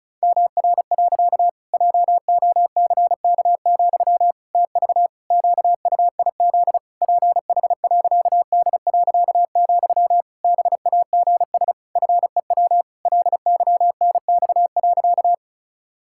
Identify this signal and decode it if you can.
Morse code, heard as a keyed tone, about 36 wpm, 700 Hz. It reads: MR. JOCK, TV QUIZ PH.D., BAGS FEW LYNX.
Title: Pangram Challenge v2 - 35wpm